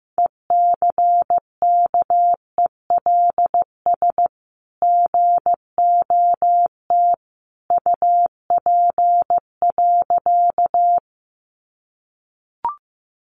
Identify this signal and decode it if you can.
Morse code, heard as a keyed tone, about 15 wpm, 700 Hz. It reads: ECKELS GOT UP.
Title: A Sound Of Thunder 15wpm